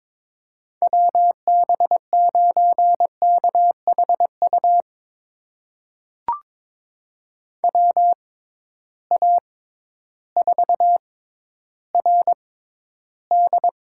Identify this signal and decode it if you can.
Morse code, heard as a keyed tone, about 22 wpm, spaced as for 12 wpm, 700 Hz. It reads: WB9KHU WA4RD
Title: USA Call Signs - 6 Characters - 6x Character Spacing - 22wpm